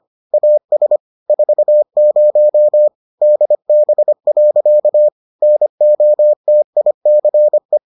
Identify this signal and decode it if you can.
Morse code, heard as a keyed tone, about 25 wpm, 600 Hz. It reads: AS 40 DB. NOTICE